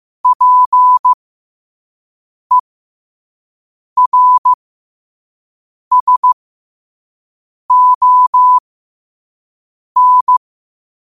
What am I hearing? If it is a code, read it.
Morse code, heard as a keyed tone, about 15 wpm, spaced as for 5 wpm, 1000 Hz. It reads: PERSON
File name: qotdfiles / 5wpm